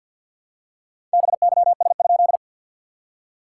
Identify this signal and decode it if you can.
Morse code, heard as a keyed tone, about 51 wpm, 690 Hz. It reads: BYS?
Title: SOTA QSO Practice 50wpm